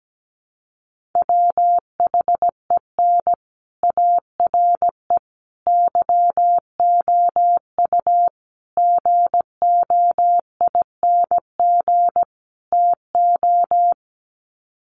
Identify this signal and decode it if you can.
Morse code, heard as a keyed tone, about 17 wpm, 700 Hz. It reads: WHEN ARE YOU GOING TO
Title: Sets of 5 Words Encore2 17wpm